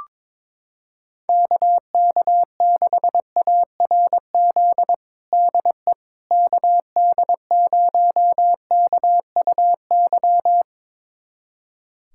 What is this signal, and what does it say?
Morse code, heard as a keyed tone, about 22 wpm, 700 Hz. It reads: KK6ARZ DE KD0KUY
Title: USA Callsign DE Callsign 22wpm